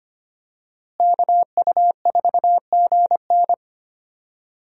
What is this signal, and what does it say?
Morse code, heard as a keyed tone, about 25 wpm, 700 Hz. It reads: KU4GN
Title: USA Call Signs - 5 Characters - Repeated 3x - 25wpm